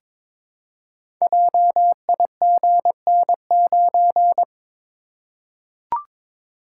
Morse code, heard as a keyed tone, about 22 wpm, 700 Hz. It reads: JIGN9
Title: Code Groups - 5 Letter:Numbers - 22wpm